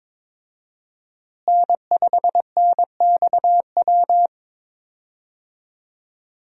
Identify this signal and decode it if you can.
Morse code, heard as a keyed tone, about 22 wpm, 700 Hz. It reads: N5NXW
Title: USA Call Signs - 5 Characters - Repeated 2x - 22wpm